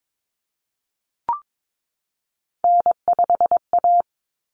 Morse code, heard as a keyed tone, about 22 wpm, 700 Hz. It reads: N5A
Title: Common Cut Numbers - 3-Digit 22wpm